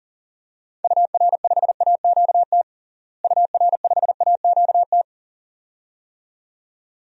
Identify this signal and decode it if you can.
Morse code, heard as a keyed tone, about 40 wpm, 700 Hz. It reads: UR5AQT UR5AQT
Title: DXCC-100 Call Signs - 6 Characters - Repeated 2x - 40wpm